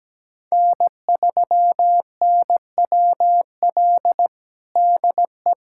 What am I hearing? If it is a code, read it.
Morse code, heard as a keyed tone, about 17 wpm, 700 Hz. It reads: N3NWL DE
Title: USA Callsign DE Callsign - 5 Characters - 17wpm